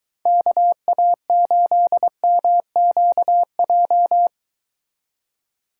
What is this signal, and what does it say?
Morse code, heard as a keyed tone, about 22 wpm, 700 Hz. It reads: KA8MQJ